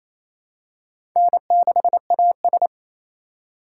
Morse code, heard as a keyed tone, about 28 wpm, 700 Hz. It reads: N6AS
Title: USA Call Signs - 4 Characters - 6x Character Spacing - 28wpm